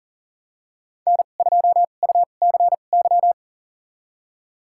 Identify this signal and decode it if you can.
Morse code, heard as a keyed tone, about 38 wpm, spaced as for 28 wpm, 700 Hz. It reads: N2UCY
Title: USA Call Signs - 5 Characters - 2x Character Spacing - 40wpm